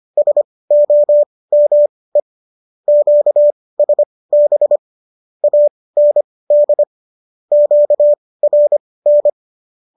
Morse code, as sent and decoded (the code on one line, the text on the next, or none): ... --- -- . / --.- ... -... / .- -. -.. / --.- .-. -.
SOME QSB AND QRN